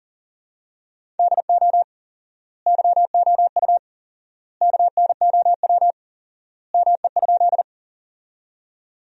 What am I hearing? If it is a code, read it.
Morse code, heard as a keyed tone, about 40 wpm, 700 Hz. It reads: DO YOU KNOW ME?
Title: 4x Word-Spacing - Sentences from Top 200 Words - 40wpm